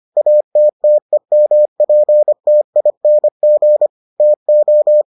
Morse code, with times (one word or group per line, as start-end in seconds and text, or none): 0.2-3.9 ATTEMPTING
4.2-5.0 TO